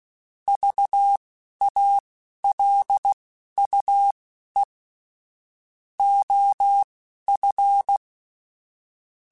Morse code, heard as a keyed tone, about 16 wpm, spaced as for 11 wpm, 800 Hz. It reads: VALUE OF
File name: Technical Subjects / Technical Subjects-mp3s / XC @10